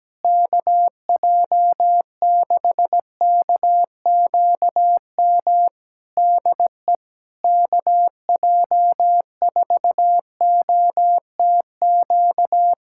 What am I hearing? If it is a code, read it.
Morse code, heard as a keyed tone, about 17 wpm, 700 Hz. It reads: KJ6KQM DE KJ4OTQ
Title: Practice QSO - Exchange 4 - Ending - 17wpm